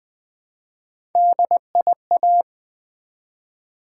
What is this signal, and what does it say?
Morse code, heard as a keyed tone, about 20 wpm, 700 Hz. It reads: DIA
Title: Build A Word - 1 Letter At A Time - 5-Letter Words - 20wpm